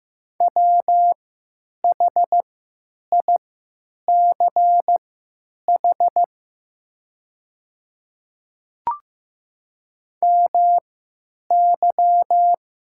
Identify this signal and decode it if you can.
Morse code, heard as a keyed tone, about 15 wpm, spaced as for 8 wpm, 700 Hz. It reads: WHICH MY
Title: 3x Character Spacing - Top 300 Words - 15wpm